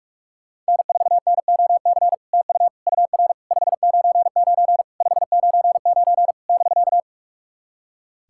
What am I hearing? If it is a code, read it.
Morse code, heard as a keyed tone, about 44 wpm, 690 Hz. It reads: N4NOC TU UR 599 599 <BK>
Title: SOTA QSO Practice 45wpm